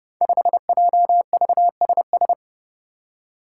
Morse code, heard as a keyed tone, about 30 wpm, 700 Hz. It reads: 5JVSS
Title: Five Challenging Letter-Number Combinations 30wpm